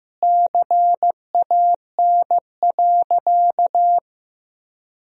Morse code, heard as a keyed tone, about 15 wpm, 700 Hz. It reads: CAN.
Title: 4x Word-Spacing - Sentences from Top 500 Words - 15wpm